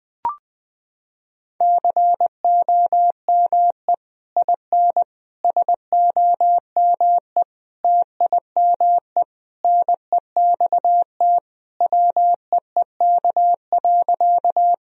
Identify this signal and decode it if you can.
Morse code, heard as a keyed tone, about 20 wpm, 700 Hz. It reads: COME IN SOME TIME NEXT WEEK.
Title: Sentences from Top 1000 Words - Review Part 2 - 20wpm